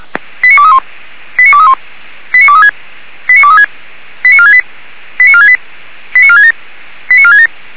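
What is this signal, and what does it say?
Five-tone selective call (ZVEI), heard as quick runs of five tones, tones of about 70 ms. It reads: 89231 89231 89236 89236 89468 89468 89467 89467
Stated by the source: Five-tone ZVEI selective call 89240 received by SDR.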